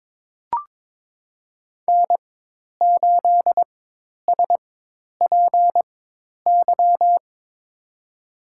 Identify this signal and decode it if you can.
Morse code, heard as a keyed tone, about 22 wpm, spaced as for 10 wpm, 700 Hz. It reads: N8SPY
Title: USA Call Signs - 5 Characters - 4x Character Spacing - 22wpm